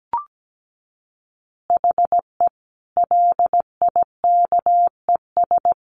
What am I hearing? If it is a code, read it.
Morse code, heard as a keyed tone, about 17 wpm, 700 Hz. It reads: HE LIKES
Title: Additional Easy Phrases 17wpm